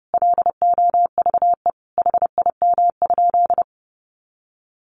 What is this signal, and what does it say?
Morse code, heard as a keyed tone, about 28 wpm, 700 Hz. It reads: LOVE HIM?
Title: Sentences from Top 400 Words 30wpm